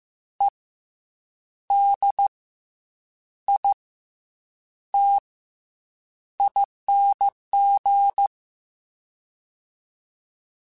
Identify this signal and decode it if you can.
Morse code, heard as a keyed tone, about 15 wpm, 800 Hz. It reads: E D I T ING
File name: Special INGs / Special INGs-mp3s / SlowToFaster ing @13